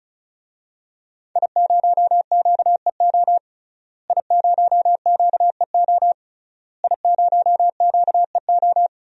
Morse code, heard as a keyed tone, about 36 wpm, 700 Hz. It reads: I0QEO I0QEO I0QEO
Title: DXCC-100 Call Signs - 5 Characters - Repeated 3x - 35wpm